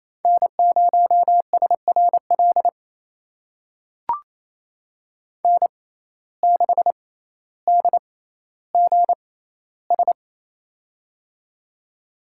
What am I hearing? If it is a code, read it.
Morse code, heard as a keyed tone, about 28 wpm, spaced as for 14 wpm, 700 Hz. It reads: N0SRL N6DGS
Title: USA Call Signs - 5 Characters - 6x Character Spacing - 28wpm